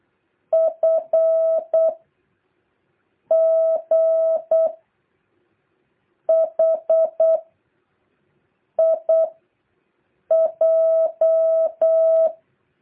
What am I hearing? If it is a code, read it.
Morse code, heard as a keyed tone, about 8 wpm, 660 Hz. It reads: F G H I J
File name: jh8cbh / 2023cw